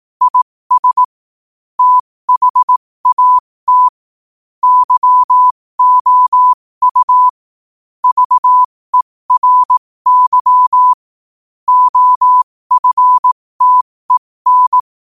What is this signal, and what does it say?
Morse code, heard as a keyed tone, about 18 wpm, spaced as for 15 wpm, 1000 Hz. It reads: IS THAT YOU VERY OFTEN
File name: qotdfiles / 15wpm